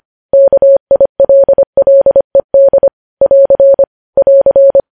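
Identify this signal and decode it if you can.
Morse code, heard as a keyed tone, about 25 wpm, 550 Hz. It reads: KILLED <AR> <AR>
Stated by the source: Morse code news headlines sent at 25.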